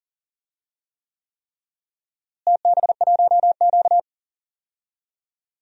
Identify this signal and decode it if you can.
Morse code, heard as a keyed tone, about 37 wpm, 700 Hz. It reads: TB1Q
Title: DXCC-100 Call Signs - 4 Characters - Repeated 3x - 40wpm